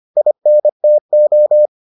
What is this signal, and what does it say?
Morse code, heard as a keyed tone, about 25 wpm, 600 Hz. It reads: INTO